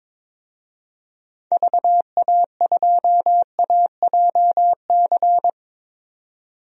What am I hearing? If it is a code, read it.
Morse code, heard as a keyed tone, about 22 wpm, 700 Hz. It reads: VA2AJC
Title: DXCC-100 Call Signs - 6 Characters - 22wpm